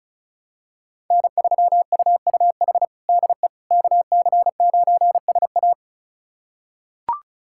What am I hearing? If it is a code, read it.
Morse code, heard as a keyed tone, about 36 wpm, 700 Hz. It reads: N3UUH DE KC9SA
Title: USA Callsign DE Callsign - 5 Characters - 35wpm